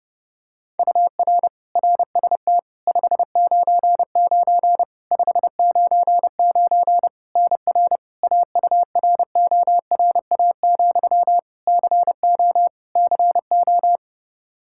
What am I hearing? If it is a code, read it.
Morse code, heard as a keyed tone, about 30 wpm, 700 Hz. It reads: UR RST 599 599 NR AURORA, CO CO
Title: K1USN SST QSO Practice 30wpm